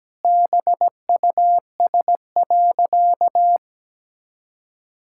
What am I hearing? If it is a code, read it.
Morse code, heard as a keyed tone, about 17 wpm, 700 Hz. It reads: BUS.
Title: Sentences from Top 1000 Words - Review Part 6 - 17wpm